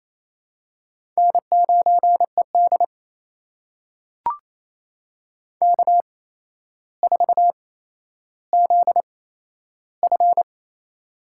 Morse code, heard as a keyed tone, about 28 wpm, spaced as for 12 wpm, 700 Hz. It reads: N9ED K4ZF
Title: USA Call Signs - 4 Characters - 8x Character Spacing - 28wpm